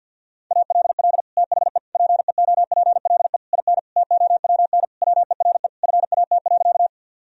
Morse code, heard as a keyed tone, about 49 wpm, 700 Hz. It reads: ALL THE PEOPLE IN TOWN WERE FAT.
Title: Sentences from Top 1000 Words - Review Part 4 - 50wpm